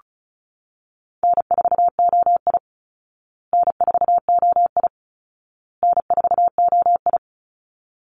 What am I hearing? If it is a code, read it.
Morse code, heard as a keyed tone, about 34 wpm, 700 Hz. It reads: N4OI N4OI N4OI
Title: Easy Call Signs Repeated 3x - 35wpm